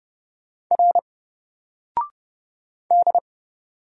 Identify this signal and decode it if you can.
Morse code, heard as a keyed tone, about 30 wpm, 700 Hz. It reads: R D
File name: Single-Letter-Number-30wpm